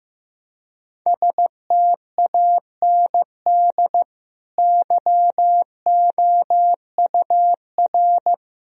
Morse code, heard as a keyed tone, about 15 wpm, 700 Hz. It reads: STAND YOUR
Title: Sentences from Top 1000 Words - Review Part 2 - 15wpm